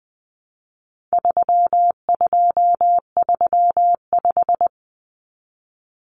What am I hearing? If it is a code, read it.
Morse code, heard as a keyed tone, about 20 wpm, 700 Hz. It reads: 3235
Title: Numbers - 4-Digit 20wpm